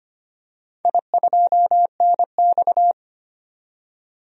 Morse code, heard as a keyed tone, about 25 wpm, 700 Hz. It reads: I2NX
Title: DXCC-100 Call Signs - 4 Characters - 25wpm